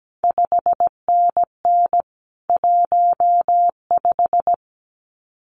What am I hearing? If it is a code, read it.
Morse code, heard as a keyed tone, about 17 wpm, 700 Hz. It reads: 5NN 15
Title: CQ WW DX Contest Exchange - 17wpm